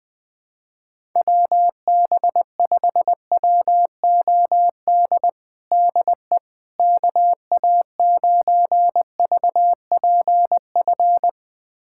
Morse code, heard as a keyed tone, about 20 wpm, 700 Hz. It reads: WB5WOD DE KA9VPF
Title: USA Callsign DE Callsign - 6 Characters - 20wpm